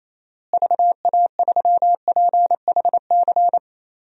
Morse code, heard as a keyed tone, about 28 wpm, 700 Hz. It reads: VA3PHC
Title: DXCC-100 Call Signs - 6 Characters - Repeated 3x - 28wpm